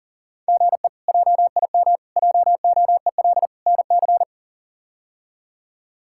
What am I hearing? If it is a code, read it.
Morse code, heard as a keyed tone, about 40 wpm, 690 Hz. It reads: GE JIM JOEL NC
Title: K1USN SST QSO Practice - 40wpm